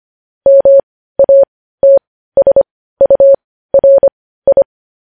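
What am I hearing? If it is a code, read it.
Morse code, heard as a keyed tone, about 25 wpm, spaced as for 15 wpm, 550 Hz. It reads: MATSURI
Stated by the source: Morse code news headlines practice at 15.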